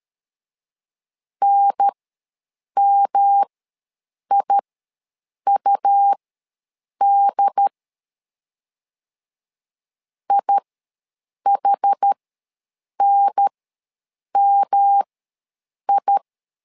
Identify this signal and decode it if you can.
Morse code, heard as a keyed tone, about 13 wpm, spaced as for 7 wpm, 800 Hz. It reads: NMIUD IHNMI
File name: LEZIONI / 3 MWRO / 3 M ws 12wpm 1745